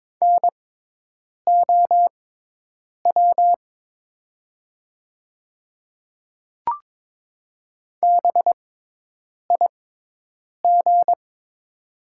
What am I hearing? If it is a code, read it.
Morse code, heard as a keyed tone, about 22 wpm, spaced as for 8 wpm, 700 Hz. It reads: NOW BIG
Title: 6x Character Spacing - Top 200 Words - 22wpm